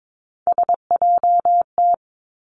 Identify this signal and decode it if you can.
Morse code, heard as a keyed tone, about 22 wpm, 700 Hz. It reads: SJT
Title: Three Challenging Letter-Number Combinations 22wpm